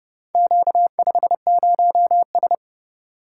Morse code, heard as a keyed tone, about 27 wpm, 700 Hz. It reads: Q50S